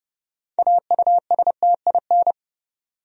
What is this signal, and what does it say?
Morse code, heard as a keyed tone, about 30 wpm, 700 Hz. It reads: AUSTIN
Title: USA Common Names - 30wpm